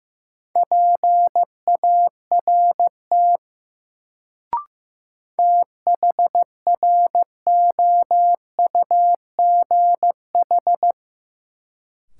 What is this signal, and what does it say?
Morse code, heard as a keyed tone, about 15 wpm, 700 Hz. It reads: PART THROUGH
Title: Sentences from Top 200 Words - New Word Drill - 15wpm